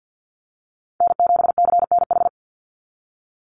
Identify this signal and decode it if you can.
Morse code, heard as a keyed tone, about 48 wpm, 700 Hz. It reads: N7CNH
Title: NA SOTA Chaser Callsigns 50wpm